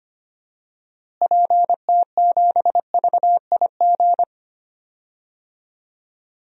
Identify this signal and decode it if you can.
Morse code, heard as a keyed tone, about 25 wpm, 700 Hz. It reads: PT7VIG
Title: DXCC-100 Call Signs - 6 Characters - Repeated 3x - 25wpm